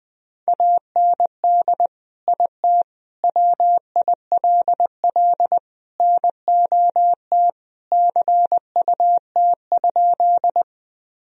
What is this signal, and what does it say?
Morse code, heard as a keyed tone, about 20 wpm, 700 Hz. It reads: AND IT WILL NOT CUT?
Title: Sentences from Top 1000 Words - Review Part 1 - 20wpm